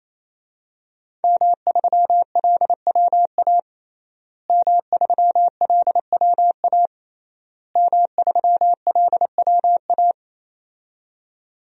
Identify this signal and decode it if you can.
Morse code, heard as a keyed tone, about 28 wpm, 700 Hz. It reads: M3LWA M3LWA M3LWA
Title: DXCC-100 Call Signs - 5 Characters - Repeated 3x - 28wpm